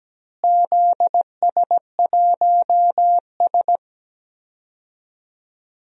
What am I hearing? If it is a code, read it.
Morse code, heard as a keyed tone, about 17 wpm, 700 Hz. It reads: ZS1S
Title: DXCC-100 Call Signs - 4 Characters - Repeated 2x - 17wpm